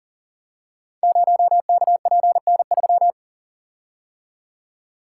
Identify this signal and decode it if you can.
Morse code, heard as a keyed tone, about 44 wpm, 690 Hz. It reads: 0KPN3
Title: Code Groups - 5 Letter:Numbers - 40wpm